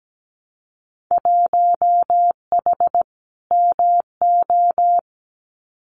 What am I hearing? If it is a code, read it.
Morse code, heard as a keyed tone, about 17 wpm, 700 Hz. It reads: 1H MO
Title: Winter Field Day Exchanges 17wpm